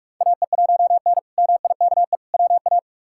Morse code, heard as a keyed tone, about 41 wpm, 700 Hz. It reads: AE1N MIKE WA
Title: K1USN SST QSO Practice - 45wpm